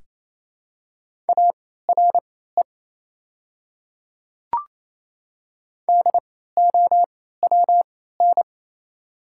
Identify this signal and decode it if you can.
Morse code, heard as a keyed tone, about 28 wpm, spaced as for 16 wpm, 700 Hz. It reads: ARE DOWN
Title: 3x Character Spacing - Top 100 Words - 28wpm